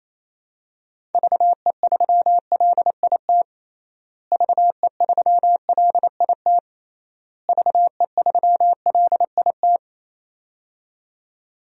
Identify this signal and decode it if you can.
Morse code, heard as a keyed tone, about 28 wpm, 700 Hz. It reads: VE3LIT VE3LIT VE3LIT
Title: DXCC-100 Call Signs - 6 Characters - Repeated 3x - 28wpm